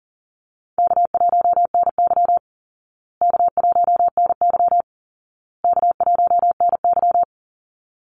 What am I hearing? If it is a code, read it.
Morse code, heard as a keyed tone, about 40 wpm, 700 Hz. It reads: K1NY K1NY K1NY
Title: Easy Call Signs Repeated 3x - 40wpm